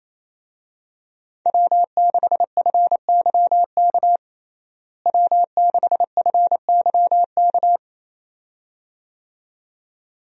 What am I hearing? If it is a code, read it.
Morse code, heard as a keyed tone, about 28 wpm, 700 Hz. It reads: W6FYK W6FYK
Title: USA Call Signs - 5 Characters - Repeated 2x - 28wpm